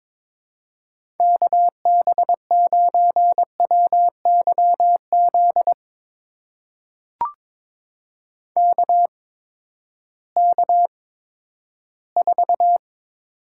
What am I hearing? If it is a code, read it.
Morse code, heard as a keyed tone, about 22 wpm, spaced as for 13 wpm, 700 Hz. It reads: KB9WYZ KK4
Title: USA Call Signs - 6 Characters - 8x Character Spacing - 22wpm